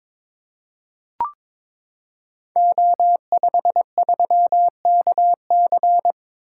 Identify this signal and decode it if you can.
Morse code, heard as a keyed tone, about 22 wpm, 700 Hz. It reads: O53KC
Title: Code Groups - 5 Letter:Numbers - 22wpm